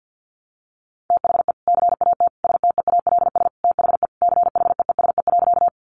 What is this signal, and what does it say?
Morse code, heard as a keyed tone, about 50 wpm, 700 Hz. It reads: THE CAT STEALS THE CHEESE.
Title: Easy Phrases 50wpm